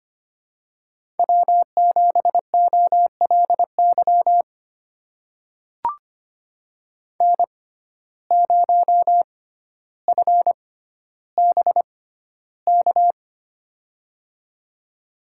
Morse code, heard as a keyed tone, about 25 wpm, spaced as for 13 wpm, 700 Hz. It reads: W7OLY N0FBK
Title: USA Call Signs - 5 Characters - 6x Character Spacing - 25wpm